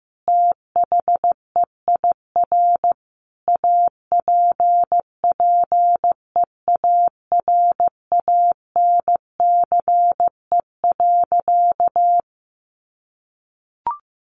Morse code, heard as a keyed tone, about 14 wpm, 700 Hz. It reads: THEIR APPEARANCE.
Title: Sentences from Top 1100 Words 15wpm